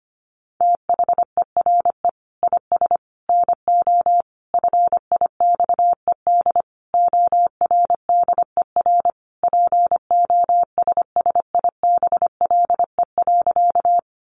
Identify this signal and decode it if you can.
Morse code, heard as a keyed tone, about 25 wpm, 700 Hz. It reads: THERE IS NO FIXED ORDER POSSIBLE.
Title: Sentences from Top 1100 Words 25wpm